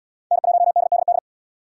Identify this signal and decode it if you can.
Morse code, heard as a keyed tone, about 69 wpm, 700 Hz. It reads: N1DDD